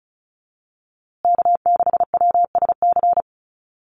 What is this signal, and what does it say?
Morse code, heard as a keyed tone, about 34 wpm, 700 Hz. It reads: K6WSC
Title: USA Call Signs Repeated 2x 35wpm